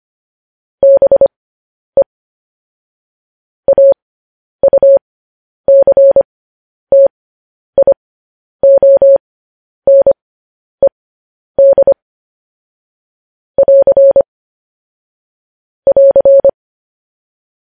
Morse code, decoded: BE AUCTIONED <AR> <AR>

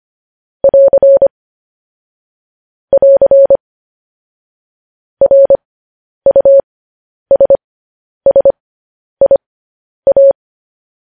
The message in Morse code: <AR> <AR> RUSSIA

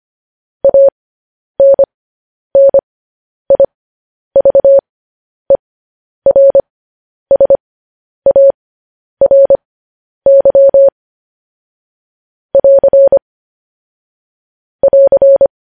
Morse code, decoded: ANNIVERSARY <AR> <AR>

